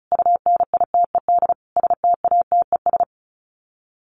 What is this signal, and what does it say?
Morse code, heard as a keyed tone, about 34 wpm, 700 Hz. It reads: UNITED STATES
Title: Sets of 5 Words 35wpm